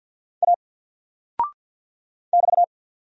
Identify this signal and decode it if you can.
Morse code, heard as a keyed tone, about 48 wpm, 700 Hz. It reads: A <BT>
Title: Single Character (Letter:Number:Punctuation:Prosign) Alt 50wpm